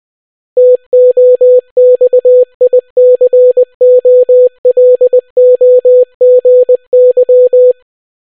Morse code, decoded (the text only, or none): TOXICOLOGY